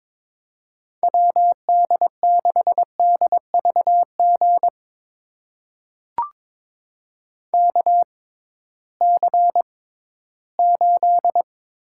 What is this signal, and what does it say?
Morse code, heard as a keyed tone, about 22 wpm, spaced as for 14 wpm, 700 Hz. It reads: WD6DVG KC8
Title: USA Call Signs - 6 Characters - 6x Character Spacing - 22wpm